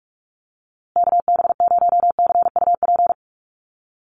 Morse code, heard as a keyed tone, about 44 wpm, 700 Hz. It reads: KB0CUP
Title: USA Call Signs Repeated 3x 45wpm